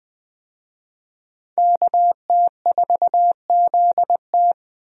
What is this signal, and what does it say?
Morse code, heard as a keyed tone, about 20 wpm, 700 Hz. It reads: KT4ZT
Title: USA Call Signs - 5 Characters - 20wpm